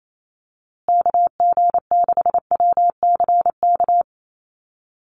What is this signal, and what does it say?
Morse code, heard as a keyed tone, about 28 wpm, 700 Hz. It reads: KG6WCK
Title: USA Call Signs Repeated 3x 28wpm